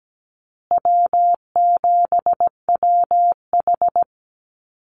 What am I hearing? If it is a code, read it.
Morse code, heard as a keyed tone, about 17 wpm, 700 Hz. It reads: W7WH
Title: USA Call Signs Repeated 2x 17wpm